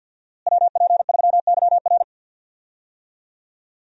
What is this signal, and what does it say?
Morse code, heard as a keyed tone, about 50 wpm, 700 Hz. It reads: WP3YR
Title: USA Call Signs - 5 Characters - Repeated 2x - 50wpm